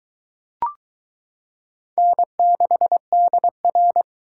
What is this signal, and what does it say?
Morse code, heard as a keyed tone, about 21 wpm, 700 Hz. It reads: N6DR